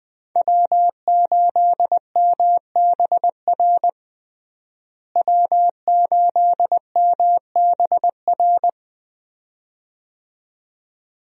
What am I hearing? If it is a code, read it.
Morse code, heard as a keyed tone, about 20 wpm, 700 Hz. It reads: W8MBR W8MBR
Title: USA Call Signs - 5 Characters - Repeated 2x - 20wpm